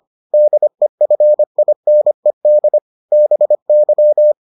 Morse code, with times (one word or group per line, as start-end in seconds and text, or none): 0.3-2.8 DEFINED
3.1-4.3 BY